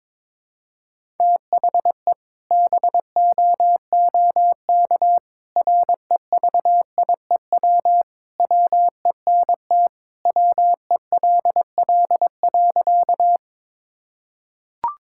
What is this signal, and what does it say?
Morse code, heard as a keyed tone, about 22 wpm, 700 Hz. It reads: THE BOOK REVIEW WENT WELL.
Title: Sentences from Top 1000 Words - Review Part 6 - 22wpm